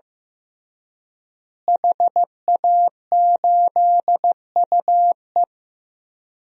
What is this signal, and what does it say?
Morse code, heard as a keyed tone, about 15 wpm, 700 Hz. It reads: HA8UE
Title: DXCC-100 Call Signs - 5 Characters - Repeated 2x - 15wpm